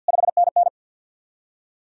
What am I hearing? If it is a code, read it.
Morse code, heard as a keyed tone, about 66 wpm, spaced as for 58 wpm, 700 Hz. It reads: 5NN
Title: Speed-Racing Common QSO Phrases - 50wpm